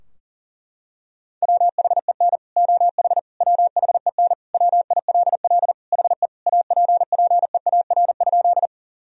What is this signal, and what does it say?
Morse code, heard as a keyed tone, about 40 wpm, 700 Hz. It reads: WHEN OH WHEN WILL HE APPEAR?
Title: Sentences from Top 1000 Words - Review Part 2 - 40wpm